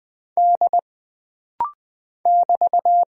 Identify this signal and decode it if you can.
Morse code, heard as a keyed tone, about 20 wpm, 700 Hz. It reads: D <BT>
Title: Single Character (Letter:Number:Punctuation:Prosign) - Rapid-Fire - 20wpm